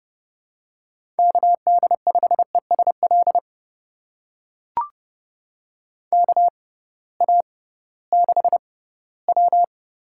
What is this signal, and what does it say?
Morse code, heard as a keyed tone, about 32 wpm, spaced as for 18 wpm, 700 Hz. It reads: KD5ESL KA6W
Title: USA Call Signs - 6 Characters - 6x Character Spacing - 30wpm